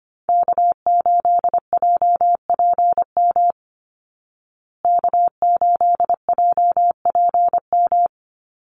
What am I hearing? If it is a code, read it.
Morse code, heard as a keyed tone, about 25 wpm, 700 Hz. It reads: K8JPM K8JPM
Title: Call Signs Repeated 3x - 25wpm